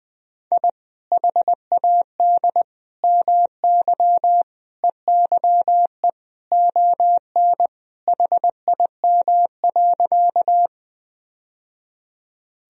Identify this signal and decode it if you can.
Morse code, heard as a keyed tone, about 21 wpm, 700 Hz. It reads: I HAD MY EYE ON HIM.
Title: Sentences from Top 1000 Words - Review Part 1 - 20wpm